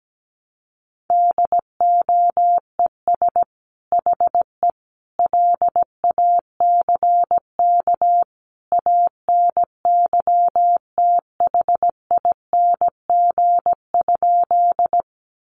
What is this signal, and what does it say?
Morse code, heard as a keyed tone, about 17 wpm, 700 Hz. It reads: DOES HE LACK ANYTHING?
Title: Sentences from Top 1100 Words 17wpm